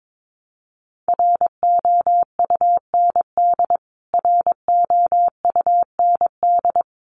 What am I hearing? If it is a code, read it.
Morse code, heard as a keyed tone, about 22 wpm, 700 Hz. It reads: ROUND ROUND
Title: Top 300 Words Repeated 22wpm